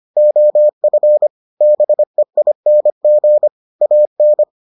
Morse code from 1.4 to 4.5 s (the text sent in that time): BEING AN